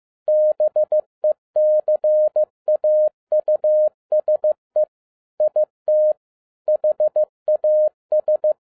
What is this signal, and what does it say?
Morse code, heard as a keyed tone, about 15 wpm, 610 Hz. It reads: BECAUSE IT HAS
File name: MY CW LIFE 1 15wpm